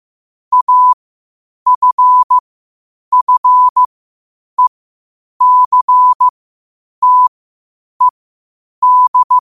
Morse code, heard as a keyed tone, about 15 wpm, spaced as for 8 wpm, 1000 Hz. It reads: AFFECTED
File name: qotdfiles / 8wpm